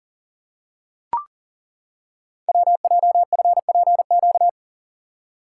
Morse code, heard as a keyed tone, about 39 wpm, 700 Hz. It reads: WJFPQ
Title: Code Groups - 5 Letters - 40wpm